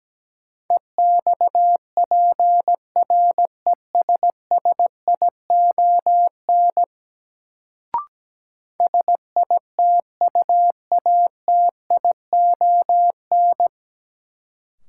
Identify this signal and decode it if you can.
Morse code, heard as a keyed tone, about 17 wpm, 700 Hz. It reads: EXPRESSION SITUATION
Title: Sentences from Top 1200 Words - New Word Drill - 17wpm